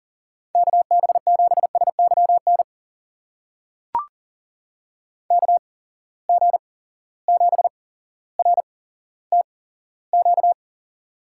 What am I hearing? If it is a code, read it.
Morse code, heard as a keyed tone, about 40 wpm, spaced as for 17 wpm, 700 Hz. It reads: KB7SYN KG7RTQ
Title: USA Call Signs - 6 Characters - 8x Character Spacing - 40wpm